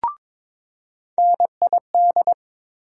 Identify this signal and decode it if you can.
Morse code, heard as a keyed tone, about 22 wpm, 700 Hz. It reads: NID
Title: USA Call Sign Postfixes - 3 Letters 22wpm